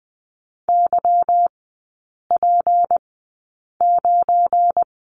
Morse code, heard as a keyed tone, about 20 wpm, spaced as for 8 wpm, 700 Hz. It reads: YP9
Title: 172 Lesson 28e - Number 9 - Three Characters 20wpm